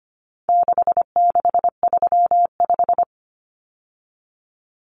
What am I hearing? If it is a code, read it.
Morse code, heard as a keyed tone, about 25 wpm, 700 Hz. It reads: 6635
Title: Numbers - 4-Digit 25wpm